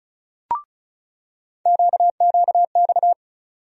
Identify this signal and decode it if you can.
Morse code, heard as a keyed tone, about 36 wpm, 700 Hz. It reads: QQX
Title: USA Call Sign Postfixes - 3 Letters 35wpm